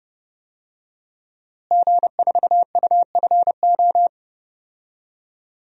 Morse code, heard as a keyed tone, about 30 wpm, 700 Hz. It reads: G4UFO
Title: DXCC-100 Call Signs - 5 Characters - Repeated 2x - 30wpm